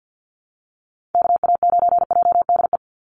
Morse code, heard as a keyed tone, about 50 wpm, 700 Hz. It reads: KA9JDE
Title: USA Call Signs 50wpm